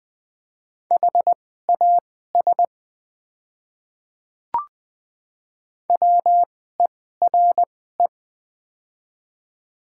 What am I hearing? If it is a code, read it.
Morse code, heard as a keyed tone, about 20 wpm, spaced as for 14 wpm, 700 Hz. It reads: HAS WERE
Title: 2x Character Spacing - Top 100 Words - 20wpm